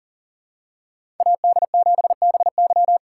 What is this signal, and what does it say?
Morse code, heard as a keyed tone, about 40 wpm, 700 Hz. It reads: AD7BY
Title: USA Call Signs - 5 Characters - 3x Character Spacing - 40wpm